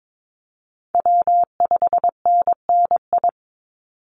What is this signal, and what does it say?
Morse code, heard as a keyed tone, about 22 wpm, 700 Hz. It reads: W5NNI
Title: USA Call Signs 22wpm